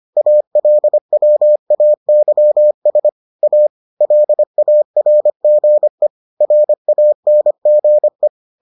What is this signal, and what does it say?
Morse code, heard as a keyed tone, about 25 wpm, 600 Hz. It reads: ALWAYS A LARGE RANGE